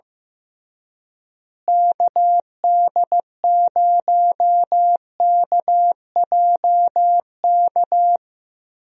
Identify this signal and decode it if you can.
Morse code, heard as a keyed tone, about 15 wpm, 700 Hz. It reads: KD0KJK
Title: USA Call Signs - 6 Characters - Repeated 3x - 15wpm